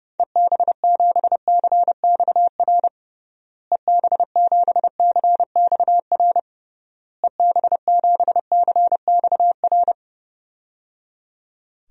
Morse code, heard as a keyed tone, about 32 wpm, 700 Hz. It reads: EB7CXR EB7CXR EB7CXR
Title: DXCC-100 Call Signs - 6 Characters - Repeated 3x - 30wpm